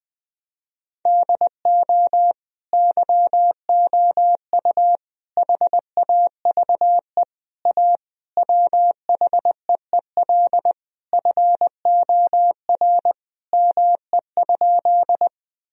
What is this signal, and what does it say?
Morse code, heard as a keyed tone, about 19 wpm, 700 Hz. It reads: DO YOU HAVE A WHEEL FOR ME?
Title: Sentences from Top 1000 Words - Review Part 3 - 20wpm